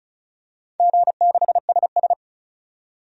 Morse code, heard as a keyed tone, about 34 wpm, 700 Hz. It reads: G6SS
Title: DXCC-100 Call Signs - 4 Characters - 35wpm